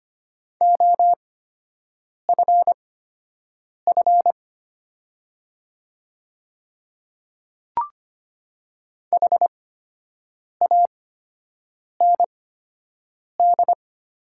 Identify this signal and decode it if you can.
Morse code, heard as a keyed tone, about 25 wpm, spaced as for 7 wpm, 690 Hz. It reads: OFF HAND
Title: 8x Character Spacing - Top 300 Words - 25wpm